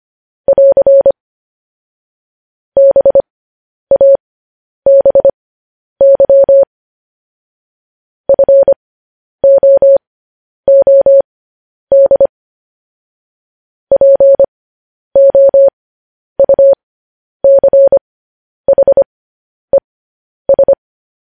Morse code, heard as a keyed tone, about 25 wpm, spaced as for 10 wpm, 550 Hz. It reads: <AR> BABY FOOD POUCHES
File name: podcast / News headlines-1x-25x10